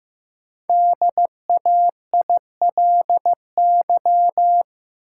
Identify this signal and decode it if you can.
Morse code, heard as a keyed tone, about 15 wpm, 700 Hz. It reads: DAILY
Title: Five Letter Words - 15wpm